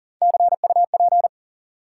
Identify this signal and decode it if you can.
Morse code, heard as a keyed tone, about 40 wpm, 700 Hz. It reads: CUP
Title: Sentences from Top 1600 Words - New Word Drill - 40wpm